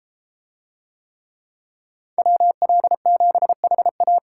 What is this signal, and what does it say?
Morse code, heard as a keyed tone, about 33 wpm, 700 Hz. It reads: WL7HA